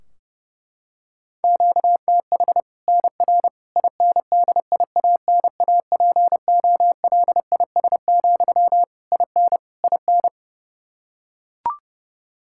Morse code, heard as a keyed tone, about 31 wpm, 700 Hz. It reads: QTH NR INDIANAPOLIS, IN IN
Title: Practice QSO - Exchange 1 - Beginning - 30wpm